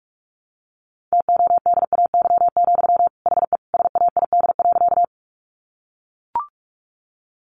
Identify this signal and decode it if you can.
Morse code, heard as a keyed tone, about 44 wpm, 700 Hz. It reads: TODAY, HE SAID.
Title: Sentences from Top 600 Words 45wpm